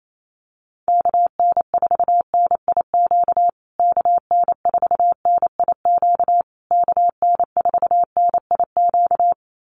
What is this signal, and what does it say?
Morse code, heard as a keyed tone, about 28 wpm, 700 Hz. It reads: KN4NIQ KN4NIQ KN4NIQ
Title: USA Call Signs Repeated 3x 28wpm